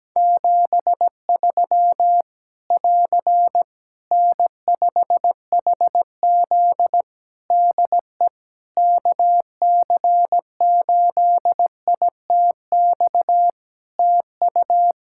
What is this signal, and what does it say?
Morse code, heard as a keyed tone, about 17 wpm, 700 Hz. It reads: 73 <AR> N5HZ DE KC8ITX TU
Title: Practice QSO - Exchange 4 - Ending - 17wpm